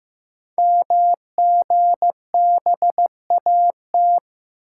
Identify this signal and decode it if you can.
Morse code, heard as a keyed tone, about 15 wpm, 700 Hz. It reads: MGBAT
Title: Code Groups - 5 Letter:Numbers - 15wpm